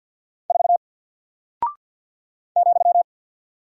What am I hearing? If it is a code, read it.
Morse code, heard as a keyed tone, about 50 wpm, 700 Hz. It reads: 4 ,
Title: Single Character (Letter:Number:Punctuation:Prosign) Alt 50wpm